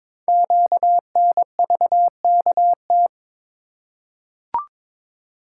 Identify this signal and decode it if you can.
Morse code, heard as a keyed tone, about 22 wpm, 700 Hz. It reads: QNVKT
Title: Code Groups - 5 Letter:Numbers - 22wpm